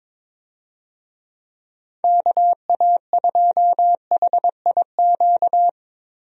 Morse code, heard as a keyed tone, about 21 wpm, 700 Hz. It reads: KA2HIQ
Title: USA Call Signs - 6 Characters - 22wpm